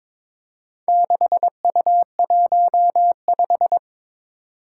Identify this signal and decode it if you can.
Morse code, heard as a keyed tone, about 23 wpm, 700 Hz. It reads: 6U15
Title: Four Challenging Letter-Number Combinations 22wpm